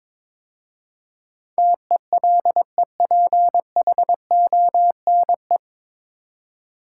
Morse code, heard as a keyed tone, about 22 wpm, 700 Hz. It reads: TELEPHONE
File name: 9 Letter Words - First 3 Letters Given - 22wpm